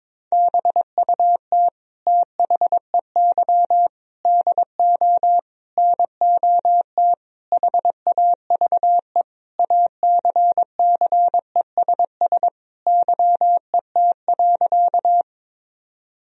Morse code, heard as a keyed tone, about 22 wpm, 700 Hz. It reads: BUT THEY DO NOT HAVE ACCESS YET.
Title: Sentences from Top 1000 Words - Review Part 4 - 22wpm